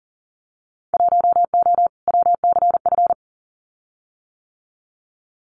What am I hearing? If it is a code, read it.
Morse code, heard as a keyed tone, about 40 wpm, 700 Hz. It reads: 1O WCF
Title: Winter Field Day Exchanges 40wpm